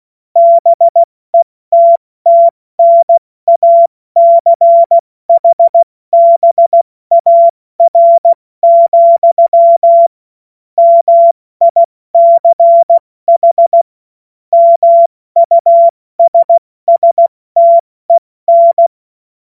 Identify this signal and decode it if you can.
Morse code, heard as a keyed tone, about 16 wpm, spaced as for 14 wpm, 680 Hz. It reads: BETTNACHBAR, MICH MUSSTEN
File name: Morsekurs / Witze / Witz 13 Februar 2024 70 F80